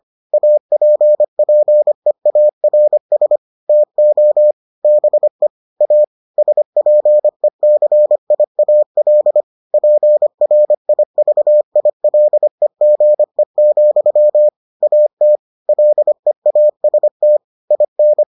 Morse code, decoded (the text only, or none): APPEARS TO BE A SPECIAL PRIVILEGE, AT LEAST IN